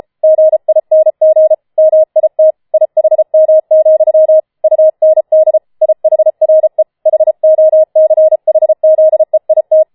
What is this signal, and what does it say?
Morse code, heard as a keyed tone, about 33 wpm, 630 Hz. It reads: GING MIT IHM, UND IHRE HOCHZEIT
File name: Schneewittchen Teil5 34wpm